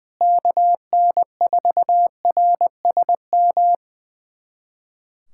Morse code, heard as a keyed tone, about 20 wpm, 700 Hz. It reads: KN4RSM
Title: USA Callsign DE Callsign 20wpm